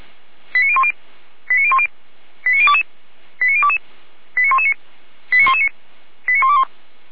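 Five-tone selective call (ZVEI), heard as quick runs of five tones, tones of about 70 ms. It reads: 89010 89010 89022 89022 89119 89119 89121
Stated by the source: ZVEI five-tone selective call received on SDR.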